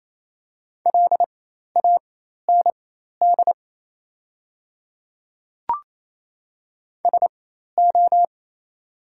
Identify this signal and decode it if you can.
Morse code, heard as a keyed tone, about 28 wpm, spaced as for 13 wpm, 690 Hz. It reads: LAND SO
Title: 4x Character Spacing - Top 200 Words - 28wpm